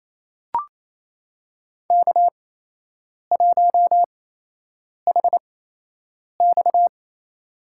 Morse code, heard as a keyed tone, about 28 wpm, spaced as for 8 wpm, 700 Hz. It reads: K1HX
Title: USA Call Signs - 4 Characters - 8x Character Spacing - 28wpm